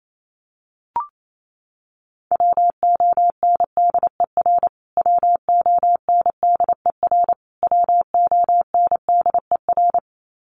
Morse code, heard as a keyed tone, about 28 wpm, 700 Hz. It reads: WONDER WONDER WONDER
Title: Top 500 Words Repeated 28wpm